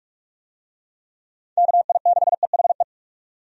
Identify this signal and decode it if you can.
Morse code, heard as a keyed tone, about 46 wpm, 700 Hz. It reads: KI6EHE
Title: USA Call Signs - 6 Characters - 45wpm